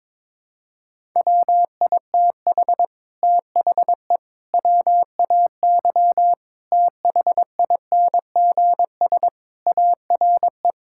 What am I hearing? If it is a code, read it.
Morse code, heard as a keyed tone, about 22 wpm, 700 Hz. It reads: WITH THE WAY THINGS ARE
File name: practice / 22wpm / Sets of 8 Words - 22wpm